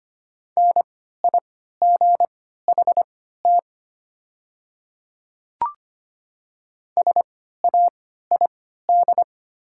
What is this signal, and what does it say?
Morse code, heard as a keyed tone, about 25 wpm, spaced as for 14 wpm, 700 Hz. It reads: NIGHT SAID
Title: 3x Character Spacing - Top 300 Words - 25wpm